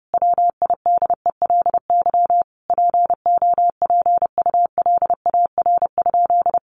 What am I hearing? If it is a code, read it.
Morse code, heard as a keyed tone, about 30 wpm, 700 Hz. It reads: WIDELY POPULAR?
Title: Sentences from Top 1200 Words 30wpm